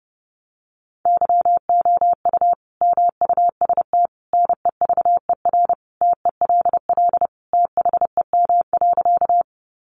Morse code, heard as a keyed tone, about 30 wpm, 700 Hz. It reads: YOU MUST NEVER TELL THEM.
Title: Sentences from Top 900 Words 30wpm